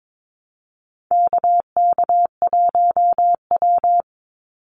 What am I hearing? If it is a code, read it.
Morse code, heard as a keyed tone, about 22 wpm, 700 Hz. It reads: KK1W
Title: NA SOTA Chaser Callsigns 22wpm